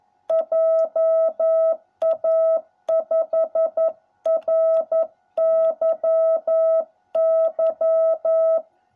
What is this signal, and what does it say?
Morse code, heard as a keyed tone, about 11 wpm, 660 Hz. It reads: JA5RYY